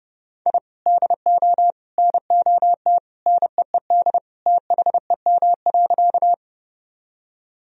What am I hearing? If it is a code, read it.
Morse code, heard as a keyed tone, about 30 wpm, 700 Hz. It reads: I DO NOT NEED THEM.
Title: Sentences from Top 1000 Words - Review Part 1 - 30wpm